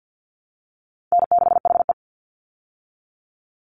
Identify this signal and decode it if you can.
Morse code, heard as a keyed tone, about 48 wpm, 700 Hz. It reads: N6HE
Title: Easy Call Signs Repeated 3x - 50wpm